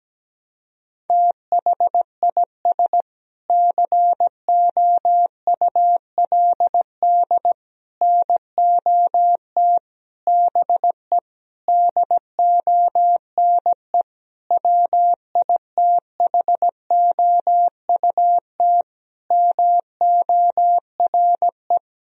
Morse code, heard as a keyed tone, about 17 wpm, 700 Hz. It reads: THIS COULD NOT BE DONE WITHOUT MORE
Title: Sentences from Top 1000 Words - Review Part 3 - 17wpm